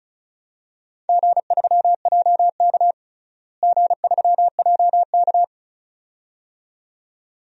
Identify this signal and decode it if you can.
Morse code, heard as a keyed tone, about 36 wpm, 700 Hz. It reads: G3JK G3JK
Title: DXCC-100 Call Signs - 4 Characters - Repeated 2x - 35wpm